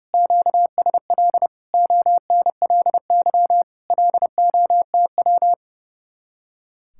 Morse code, heard as a keyed tone, about 40 wpm, 700 Hz. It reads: QSL ONLY LOTW